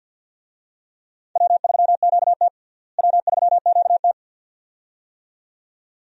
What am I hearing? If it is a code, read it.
Morse code, heard as a keyed tone, about 52 wpm, 700 Hz. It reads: W3QT W3QT
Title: USA Call Signs - 4 Characters - Repeated 2x - 50wpm